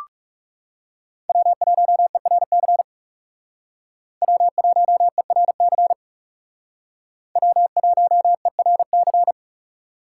Morse code, heard as a keyed tone, about 34 wpm, 700 Hz. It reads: W1ERC W1ERC W1ERC
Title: Speed Racing - USA 1x3 Call Signs 30wpm